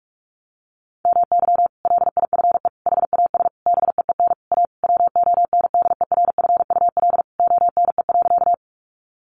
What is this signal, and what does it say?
Morse code, heard as a keyed tone, about 46 wpm, 700 Hz. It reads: MY LIFE HAS BEEN A WONDERFUL ONE.
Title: Sentences from Top 1100 Words 45wpm